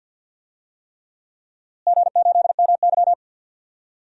Morse code, heard as a keyed tone, about 50 wpm, 700 Hz. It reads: G8MY
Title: DXCC-100 Call Signs - 4 Characters - Repeated 3x - 50wpm